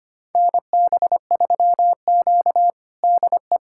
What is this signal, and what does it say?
Morse code, heard as a keyed tone, about 25 wpm, 700 Hz. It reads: NB3Q DE
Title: USA Callsign DE Callsign - 4 Characters - 25wpm